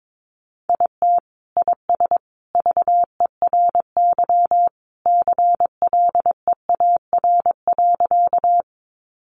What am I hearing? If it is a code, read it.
Morse code, heard as a keyed tone, about 22 wpm, 700 Hz. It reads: IT IS VERY CLEAR.
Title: Sentences from Top 500 Words 22wpm